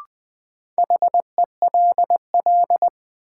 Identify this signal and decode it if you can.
Morse code, heard as a keyed tone, about 20 wpm, 700 Hz. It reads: HELL
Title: Sets of 5 Words Encore - New Word Drill - 20wpm